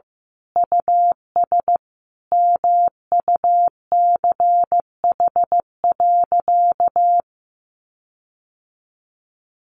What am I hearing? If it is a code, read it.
Morse code, heard as a keyed tone, about 15 wpm, 700 Hz. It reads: US MUCH.
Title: Sentences from Top 1500 Words 15wpm